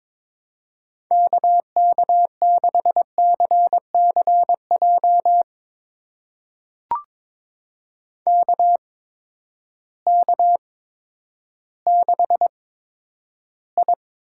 Morse code, heard as a keyed tone, about 22 wpm, spaced as for 11 wpm, 700 Hz. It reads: KK6CCJ KK6I
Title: USA Call Signs - 6 Characters - 8x Character Spacing - 22wpm